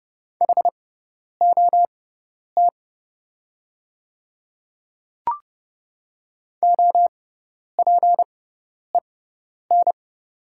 Morse code, heard as a keyed tone, about 29 wpm, spaced as for 10 wpm, 700 Hz. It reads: HOT OPEN
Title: 6x Character Spacing - Top 300 Words - 30wpm